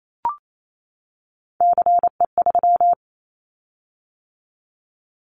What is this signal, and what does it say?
Morse code, heard as a keyed tone, about 28 wpm, 700 Hz. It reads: CE3
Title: Callsign Prefixes for Top 100 DXCC - 28wpm